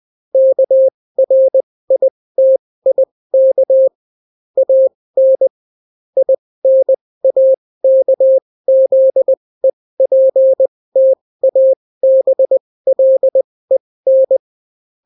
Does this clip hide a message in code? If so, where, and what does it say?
Yes: Morse code from 0.3 to 14.4 s, heard as a keyed tone, about 20 wpm, spaced as for 16 wpm, 530 Hz. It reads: KRITIK AN INAKZEPTABLEN